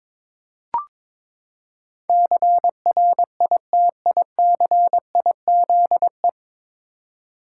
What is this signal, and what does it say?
Morse code, heard as a keyed tone, about 22 wpm, 700 Hz. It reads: CRITICIZE
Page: Morse Code Ninja practice 22wpm